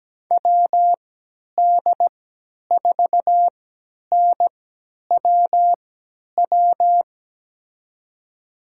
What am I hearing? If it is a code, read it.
Morse code, heard as a keyed tone, about 17 wpm, spaced as for 10 wpm, 700 Hz. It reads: WD4NWW
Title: USA Call Signs - 6 Characters - 3x Character Spacing - 17wpm